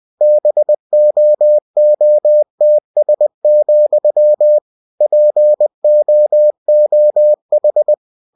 Morse code, heard as a keyed tone, about 20 wpm, 600 Hz. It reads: BOOTS, POOH
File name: cw-books / winnie-the-pooh / winnie-20wpm-page-0076